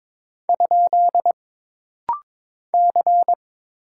Morse code, heard as a keyed tone, about 22 wpm, 700 Hz. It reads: ? C
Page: This rapid-fire timing is geared to push your ICR (Instant Character Recognition) skills.